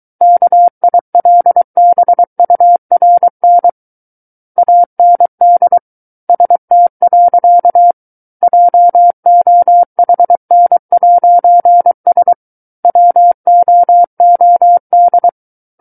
Morse code, heard as a keyed tone, about 23 wpm, 700 Hz. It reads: KILBURN AND ST. JOHN'S WOOD
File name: Book One / 14-Chapter 14 - 23 WPM